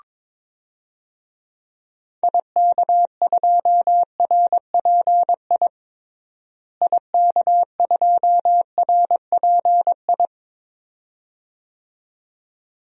Morse code, heard as a keyed tone, about 22 wpm, 700 Hz. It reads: IK2RPI IK2RPI
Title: DXCC-100 Call Signs - 6 Characters - Repeated 2x - 22wpm